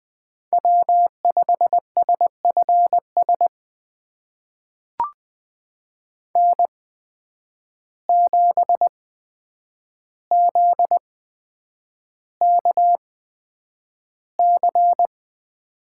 Morse code, heard as a keyed tone, about 20 wpm, spaced as for 9 wpm, 700 Hz. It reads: W5SFS N7ZKC
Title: USA Call Signs - 5 Characters - 8x Character Spacing - 20wpm